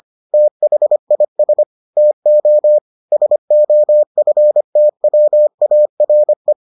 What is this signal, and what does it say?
Morse code, heard as a keyed tone, about 25 wpm, 600 Hz. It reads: THIS TO SOFTWARE